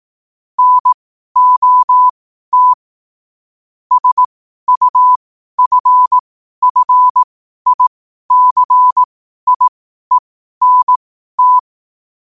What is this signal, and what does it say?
Morse code, heard as a keyed tone, about 18 wpm, spaced as for 12 wpm, 1000 Hz. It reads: NOT SUFFICIENT
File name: qotdfiles / 12wpm